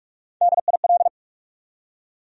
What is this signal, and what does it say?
Morse code, heard as a keyed tone, about 46 wpm, 700 Hz. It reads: DIL
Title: USA Call Sign Postfixes - 3 Letters 45wpm